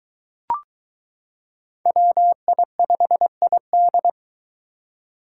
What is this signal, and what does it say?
Morse code, heard as a keyed tone, about 23 wpm, 700 Hz. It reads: WI5ID